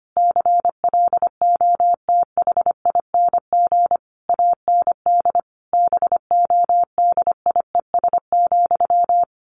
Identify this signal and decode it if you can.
Morse code, heard as a keyed tone, about 25 wpm, 700 Hz. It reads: CLOTHING AND BODIES,
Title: A Sound Of Thunder 25wpm